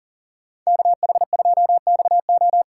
Morse code, heard as a keyed tone, about 49 wpm, 700 Hz. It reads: KH2XO
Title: Speed Racing - USA 2x2 Call Signs 40wpm